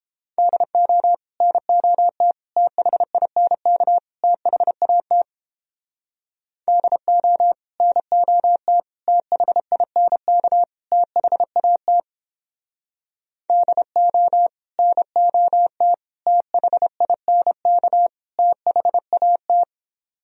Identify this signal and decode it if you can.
Morse code, heard as a keyed tone, about 29 wpm, 700 Hz. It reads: DO NOT THINK THAT DO NOT THINK THAT DO NOT THINK THAT